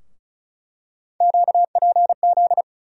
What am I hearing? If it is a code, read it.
Morse code, heard as a keyed tone, about 36 wpm, 700 Hz. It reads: QPZ
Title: USA Call Sign Postfixes - 3 Letters 35wpm